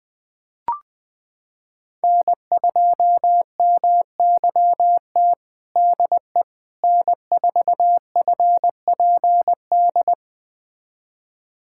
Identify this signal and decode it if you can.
Morse code, heard as a keyed tone, about 20 wpm, 700 Hz. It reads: N2MYT DE N4FPD
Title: USA Callsign DE Callsign - 5 Characters - 20wpm